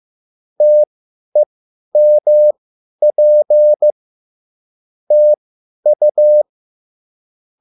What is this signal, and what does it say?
Morse code, heard as a keyed tone, about 15 wpm, spaced as for 10 wpm, 600 Hz. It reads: TEMP TU